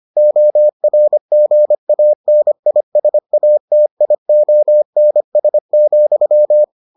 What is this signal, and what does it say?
Morse code, heard as a keyed tone, about 25 wpm, 600 Hz. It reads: ORGANISATIONS,